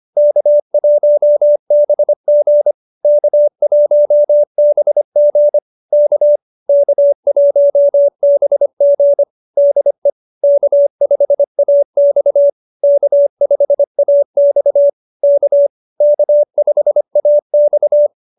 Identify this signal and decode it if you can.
Morse code, heard as a keyed tone, about 25 wpm, 590 Hz. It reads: K1BG K1BG K K1BG DE K5AX K5AX K K5AX